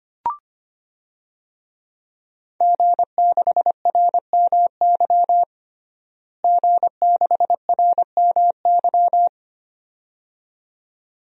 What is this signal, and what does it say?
Morse code, heard as a keyed tone, about 25 wpm, 700 Hz. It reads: G6RMY G6RMY
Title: DXCC-100 Call Signs - 5 Characters - Repeated 2x - 25wpm